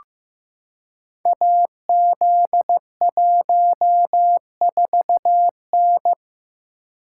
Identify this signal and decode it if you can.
Morse code, heard as a keyed tone, about 15 wpm, 700 Hz. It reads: AZ14N
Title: Code Groups - 5 Letter:Numbers - 15wpm